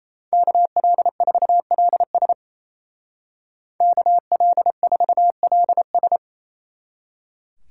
Morse code, heard as a keyed tone, about 33 wpm, 700 Hz. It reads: KL4LS KL4LS
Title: Speed Racing - USA 2x2 Call Signs 28wpm